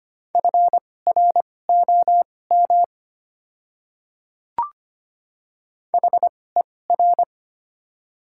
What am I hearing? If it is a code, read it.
Morse code, heard as a keyed tone, about 25 wpm, spaced as for 18 wpm, 690 Hz. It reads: FROM HER
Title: 2x Character Spacing - Top 100 Words - 25wpm